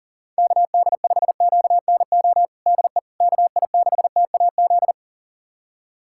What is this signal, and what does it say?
Morse code, heard as a keyed tone, about 40 wpm, 700 Hz. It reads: KD5QNO DE KI6TAZ
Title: USA Callsign DE Callsign 40wpm